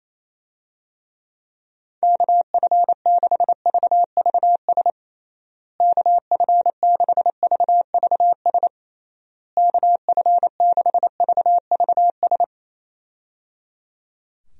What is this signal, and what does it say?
Morse code, heard as a keyed tone, about 28 wpm, 700 Hz. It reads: KF6VVS KF6VVS KF6VVS
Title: USA Call Signs - 6 Characters - Repeated 3x - 28wpm